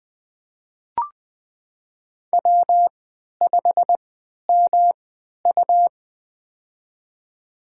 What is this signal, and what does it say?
Morse code, heard as a keyed tone, about 20 wpm, spaced as for 11 wpm, 700 Hz. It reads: W5MU
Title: USA Call Signs - 4 Characters - 3x Character Spacing - 20wpm